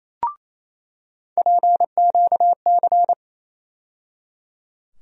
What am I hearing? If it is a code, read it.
Morse code, heard as a keyed tone, about 27 wpm, 700 Hz. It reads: PQC
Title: USA Call Sign Postfixes - 3 Letters 28wpm